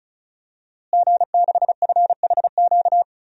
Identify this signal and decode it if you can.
Morse code, heard as a keyed tone, about 36 wpm, 700 Hz. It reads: G6FHQ
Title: DXCC-100 Call Signs - 5 Characters - Repeated 3x - 35wpm